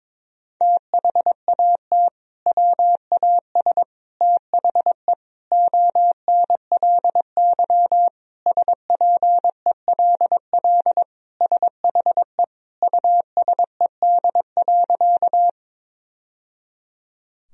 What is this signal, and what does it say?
Morse code, heard as a keyed tone, about 22 wpm, 700 Hz. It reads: THAT WAS THE ONLY SPELL SHE USED.
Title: Sentences from Top 1000 Words - Review Part 1 - 22wpm